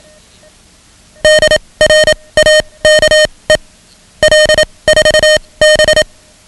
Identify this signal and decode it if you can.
Morse code, heard as a keyed tone, about 27 wpm, spaced as for 21 wpm, 620 Hz. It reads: DRAKE L4B
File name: ham radio / CodeQuick / Code Quick 10